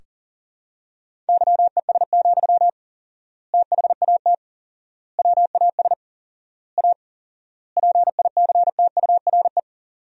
Morse code, heard as a keyed tone, about 40 wpm, 700 Hz. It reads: YES, THAT WAS A PICTURE
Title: 4x Word-Spacing - Sentences from Top 500 Words - 40wpm